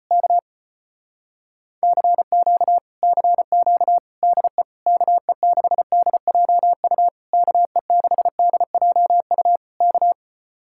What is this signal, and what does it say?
Morse code, heard as a keyed tone, about 33 wpm, 700 Hz. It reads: K CQ CQ DE KE6DJU KE6DJU K